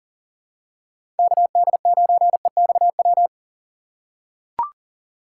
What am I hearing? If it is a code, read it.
Morse code, heard as a keyed tone, about 37 wpm, 700 Hz. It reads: KD9EXW
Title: USA Call Signs - 6x Character Spacing - 40wpm